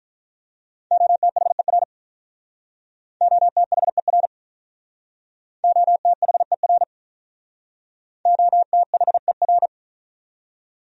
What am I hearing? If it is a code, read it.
Morse code, heard as a keyed tone, about 44 wpm, 700 Hz. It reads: OTHER OTHER OTHER OTHER
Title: Speed-Racing Top 100 Words 35wpm